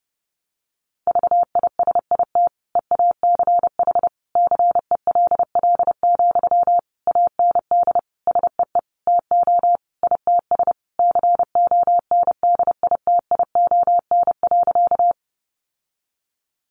Morse code, heard as a keyed tone, about 28 wpm, 700 Hz. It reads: VISIT EACH CELL, AND SEE TO ITS CONDITION.
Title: Sentences from Top 1100 Words 30wpm